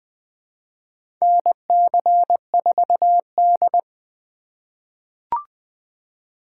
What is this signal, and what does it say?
Morse code, heard as a keyed tone, about 21 wpm, 700 Hz. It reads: NC4D
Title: USA Call Signs - 4 Characters - 6x Character Spacing - 20wpm